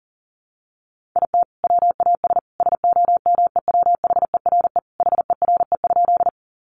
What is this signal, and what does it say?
Morse code, heard as a keyed tone, about 40 wpm, 700 Hz. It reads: IT WAS SOMEWHERE HERE?
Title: Sentences from Top 1400 Words 40wpm